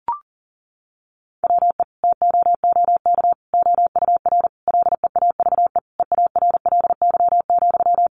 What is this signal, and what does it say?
Morse code, heard as a keyed tone, about 40 wpm, 700 Hz. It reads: WE TOOK OUR LEAVE EARLY,
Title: Sentences from Top 400 Words 40wpm